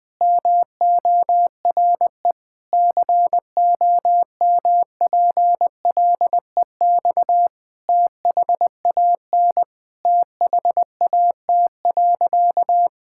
Morse code, heard as a keyed tone, about 21 wpm, 700 Hz. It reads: MORE COMPLEX THAN THAT.
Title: Sentences from Top 1000 Words - Review Part 6 - 20wpm